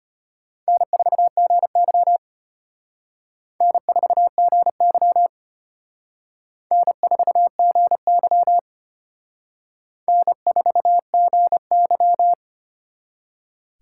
Morse code, heard as a keyed tone, about 32 wpm, 700 Hz. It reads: N4GY N4GY N4GY N4GY